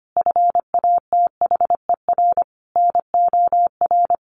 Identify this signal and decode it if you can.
Morse code, heard as a keyed tone, about 25 wpm, 700 Hz. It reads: FATHER NOR
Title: Sentences from Top 1500 Words 25wpm